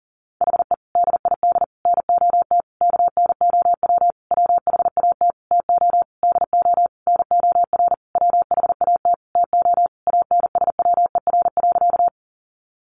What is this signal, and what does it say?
Morse code, heard as a keyed tone, about 40 wpm, 700 Hz. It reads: HE DID NOT KNOW WHAT TO DO NOR WHAT TO ANSWER.
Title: Sentences from Top 1500 Words 40wpm